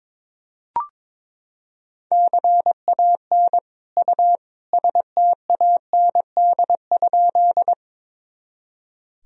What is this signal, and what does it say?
Morse code, heard as a keyed tone, about 22 wpm, 700 Hz. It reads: CAN U STAND?
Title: Sentences from Top 300 Words and Abbreviations 22wpm